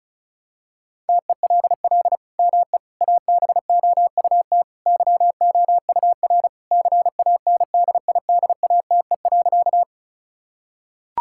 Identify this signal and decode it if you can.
Morse code, heard as a keyed tone, about 36 wpm, 700 Hz. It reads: TELL ME ABOUT YOUR CANDIDATE.
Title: Sentences from Top 1600 Words 35wpm